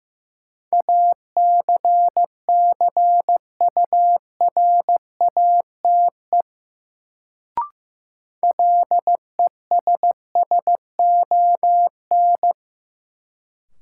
Morse code, heard as a keyed tone, about 15 wpm, 700 Hz. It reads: ACCURATE LESSON
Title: Sentences from Top 1200 Words - New Word Drill - 15wpm